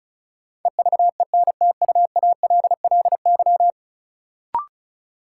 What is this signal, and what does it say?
Morse code, heard as a keyed tone, about 34 wpm, 700 Hz. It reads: EVENTUALLY
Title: Sentences from Top 1000 Words - Review Part 6 - New Word Drill - 35wpm